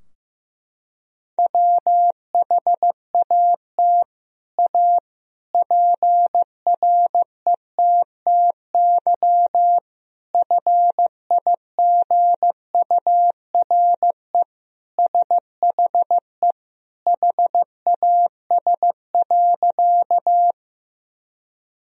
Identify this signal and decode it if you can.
Morse code, heard as a keyed tone, about 15 wpm, 700 Hz. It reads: WHAT A PRETTY FIGURE SHE HAS.
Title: Sentences from Top 1000 Words - Review Part 2 - 15wpm